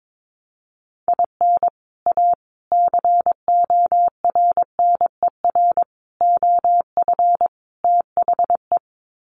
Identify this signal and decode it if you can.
Morse code, heard as a keyed tone, about 22 wpm, 700 Hz. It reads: IN A CORNER OF THE
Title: Sets of 5 Words Encore2 22wpm